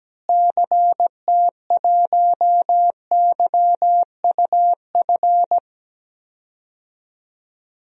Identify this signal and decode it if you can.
Morse code, heard as a keyed tone, about 17 wpm, 700 Hz. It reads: CT1YUF
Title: DXCC-100 Call Signs - 6 Characters - Repeated 3x - 17wpm